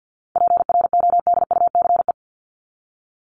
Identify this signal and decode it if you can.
Morse code, heard as a keyed tone, about 48 wpm, 700 Hz. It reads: PRODUCE
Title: ING Suffix - Top 100 Words with Base Words 50wpm